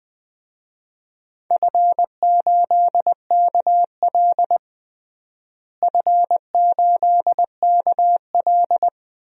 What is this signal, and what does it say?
Morse code, heard as a keyed tone, about 21 wpm, 700 Hz. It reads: F8KL F8KL
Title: DXCC-100 Call Signs - 4 Characters - Repeated 3x - 20wpm